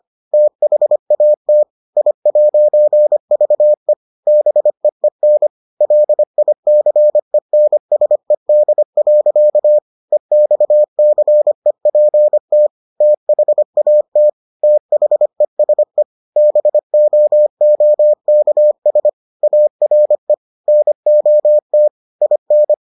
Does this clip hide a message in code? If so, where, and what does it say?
Yes: Morse code from 0.3 to 22.7 s, heard as a keyed tone, about 25 wpm, 600 Hz. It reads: THAT I'VE BEEN LICENSED. EXCEPT THAT THESE BOOKS ARE NOT IN